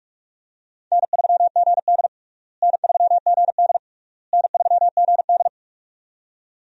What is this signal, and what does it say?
Morse code, heard as a keyed tone, about 46 wpm, 700 Hz. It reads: N3GD N3GD N3GD
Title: USA Call Signs - 4 Characters - Repeated 3x - 45wpm